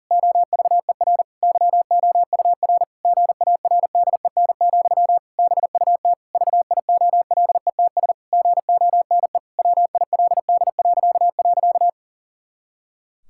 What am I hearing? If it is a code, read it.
Morse code, heard as a keyed tone, about 40 wpm, 700 Hz. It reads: OVER YOUR GARDEN, BUT VIOLETS GONE WILD..
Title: Pangram Challenge v2 - 40wpm